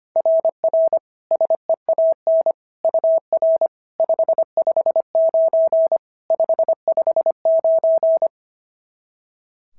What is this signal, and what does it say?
Morse code, heard as a keyed tone, about 25 wpm, 690 Hz. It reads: RR SEAN UR 559 559
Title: SOTA QSO Practice 25wpm